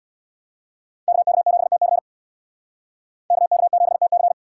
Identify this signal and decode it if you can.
Morse code, heard as a keyed tone, about 65 wpm, 700 Hz. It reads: KK7TC KK7TC